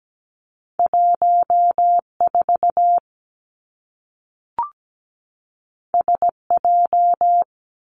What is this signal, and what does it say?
Morse code, heard as a keyed tone, about 17 wpm, 700 Hz. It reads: 1 4 S J
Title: Two Challenging Letter-Number Combinations 17wpm